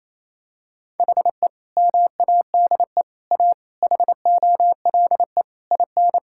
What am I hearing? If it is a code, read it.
Morse code, heard as a keyed tone, about 28 wpm, 700 Hz. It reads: HE MADE A HOLE IN
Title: Sentences from Top 1600 Words 28wpm